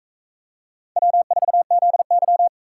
Morse code, heard as a keyed tone, about 40 wpm, 700 Hz. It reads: W4ZY